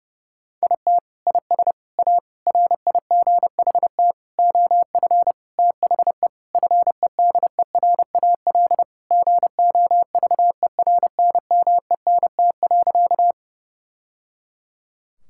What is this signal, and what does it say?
Morse code, heard as a keyed tone, about 32 wpm, 700 Hz. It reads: IT IS A RIGHT OF THE FEDERAL GOVERNMENT.
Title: Sentences from Top 1000 Words - Review Part 5 - 30wpm